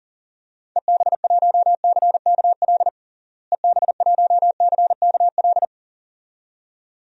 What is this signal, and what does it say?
Morse code, heard as a keyed tone, about 37 wpm, 700 Hz. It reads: EB1CKL EB1CKL
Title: DXCC-100 Call Signs - 6 Characters - Repeated 2x - 40wpm